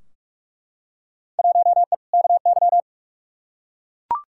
Morse code, heard as a keyed tone, about 41 wpm, 700 Hz. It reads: 1E KY
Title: Field Day Exchanges - 45wpm